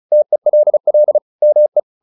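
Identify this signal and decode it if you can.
Morse code, heard as a keyed tone, about 34 wpm, 600 Hz. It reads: TELL ME
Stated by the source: Morse practice at 35 wpm from CW med Gnister, sending Ken Follett's novel Never in English.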